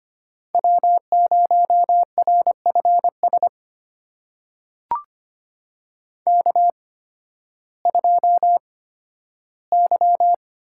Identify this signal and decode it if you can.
Morse code, heard as a keyed tone, about 25 wpm, spaced as for 13 wpm, 700 Hz. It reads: W0RFS K2Y
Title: USA Call Signs - 5 Characters - 8x Character Spacing - 25wpm